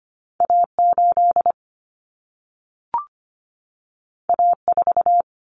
Morse code, heard as a keyed tone, about 25 wpm, 700 Hz. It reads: A8 A4
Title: Common Cut Numbers - 2-Digit 25wpm